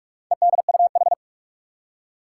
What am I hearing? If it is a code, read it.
Morse code, heard as a keyed tone, about 46 wpm, 700 Hz. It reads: EDUH
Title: Four Challenging Letter-Number Combinations 45wpm